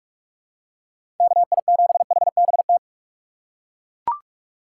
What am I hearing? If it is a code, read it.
Morse code, heard as a keyed tone, about 41 wpm, 700 Hz. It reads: KI7HBT
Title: USA Call Signs - 6 Characters - 3x Character Spacing - 45wpm